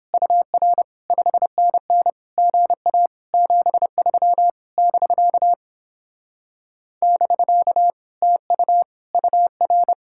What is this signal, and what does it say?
Morse code, heard as a keyed tone, about 29 wpm, 700 Hz. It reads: UR 5NN GA 73 <BK> <BK> TU UR